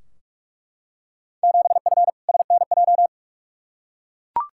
Morse code, heard as a keyed tone, about 41 wpm, 700 Hz. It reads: 7F SNJ
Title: Field Day Exchanges Encore - 45wpm